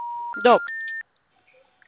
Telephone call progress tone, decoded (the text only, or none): special information tone